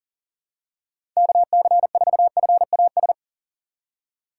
Morse code, heard as a keyed tone, about 40 wpm, 700 Hz. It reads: KC4FAS
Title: USA Call Signs - 6 Characters - 2x Character Spacing - 40wpm